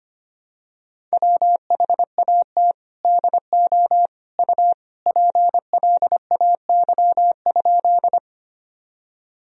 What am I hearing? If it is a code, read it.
Morse code, heard as a keyed tone, about 25 wpm, 700 Hz. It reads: WHAT DO U PLAY?
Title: Sentences from Top 200 Words and Abbreviations 25wpm